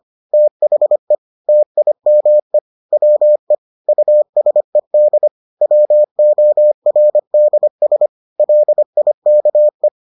Morse code, heard as a keyed tone, about 25 wpm, 600 Hz. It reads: THE TIME WE USED WORDS LIKE